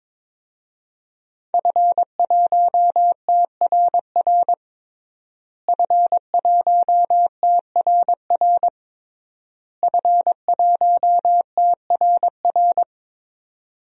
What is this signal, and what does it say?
Morse code, heard as a keyed tone, about 22 wpm, 700 Hz. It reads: F1TRR F1TRR F1TRR
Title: DXCC-100 Call Signs - 5 Characters - Repeated 3x - 22wpm